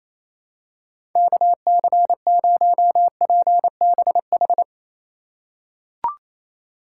Morse code, heard as a keyed tone, about 28 wpm, 700 Hz. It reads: KC0PBH
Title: USA Call Signs - 6 Characters - 4x Character Spacing - 28wpm